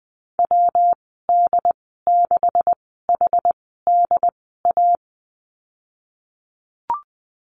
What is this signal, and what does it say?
Morse code, heard as a keyed tone, about 20 wpm, spaced as for 14 wpm, 700 Hz. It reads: WD6HDA
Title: 161 Lesson 26g - Letter g - Call Signs 20wpm